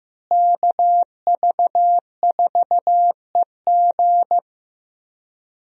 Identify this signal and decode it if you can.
Morse code, heard as a keyed tone, about 15 wpm, 700 Hz. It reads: KV4EG
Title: USA Call Signs - 5 Characters - Repeated 3x - 15wpm